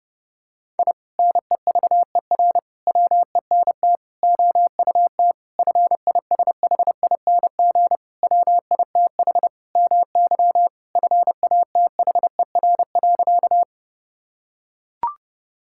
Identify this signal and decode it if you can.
Morse code, heard as a keyed tone, about 30 wpm, 700 Hz. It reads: I NEVER WENT OUT FISHING WITH MY FATHER.
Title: Sentences from Top 1000 Words - Review Part 5 - 30wpm